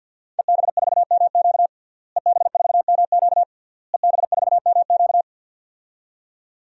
Morse code, heard as a keyed tone, about 47 wpm, 700 Hz. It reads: EB4MQ EB4MQ EB4MQ
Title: DXCC-100 Call Signs - 5 Characters - Repeated 3x - 50wpm